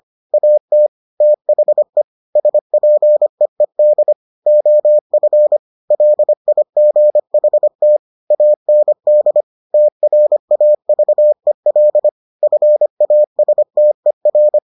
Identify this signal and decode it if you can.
Morse code, heard as a keyed tone, about 25 wpm, 600 Hz. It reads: AT THE SPEED OF LIGHT AND TRAVEL FASTER